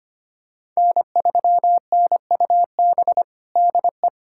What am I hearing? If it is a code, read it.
Morse code, heard as a keyed tone, about 25 wpm, 700 Hz. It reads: N3NUB DE
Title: USA Callsign DE Callsign - 5 Characters - 25wpm